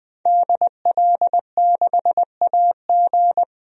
Morse code, heard as a keyed tone, about 20 wpm, 700 Hz. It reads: DL6AG
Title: DXCC-100 Call Signs - 5 Characters - 20wpm